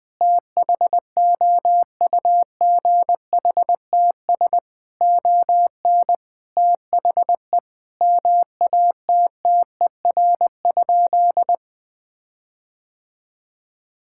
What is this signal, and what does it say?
Morse code, heard as a keyed tone, about 20 wpm, 700 Hz. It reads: THOUGHTS ON THE MATTER?
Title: Sentences with 6-Word Phrases - Spoken Beginning - 20wpm